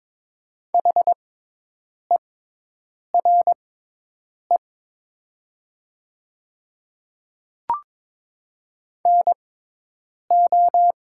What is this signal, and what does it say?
Morse code, heard as a keyed tone, about 22 wpm, spaced as for 8 wpm, 700 Hz. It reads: HERE NO
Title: 6x Character Spacing - Top 200 Words - 22wpm